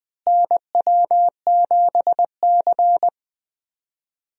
Morse code, heard as a keyed tone, about 20 wpm, 700 Hz. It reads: NW7C
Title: USA Call Signs - 4 Characters - Repeated 3x - 20wpm